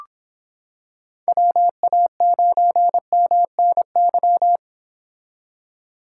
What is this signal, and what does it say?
Morse code, heard as a keyed tone, about 25 wpm, 700 Hz. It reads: WA9MNY